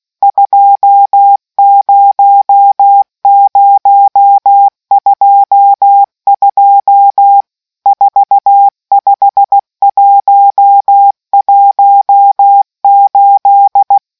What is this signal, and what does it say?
Morse code, heard as a keyed tone, about 16 wpm, 800 Hz. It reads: 20022 45118